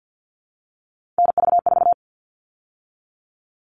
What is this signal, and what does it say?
Morse code, heard as a keyed tone, about 50 wpm, 700 Hz. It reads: NV4
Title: All Cut Numbers - 3-Digit 50wpm